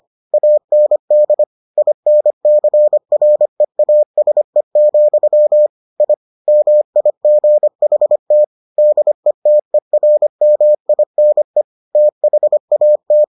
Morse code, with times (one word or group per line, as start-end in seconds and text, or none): 0.3-1.4 AND
1.8-5.7 INCREASE,
6.0-6.1 I
6.5-8.4 MIGHT
8.8-11.6 DETERMINE
11.9-13.2 THAT